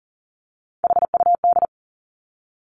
Morse code, heard as a keyed tone, about 40 wpm, 700 Hz. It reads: HUD
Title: Three Challenging Letter-Number Combinations 40wpm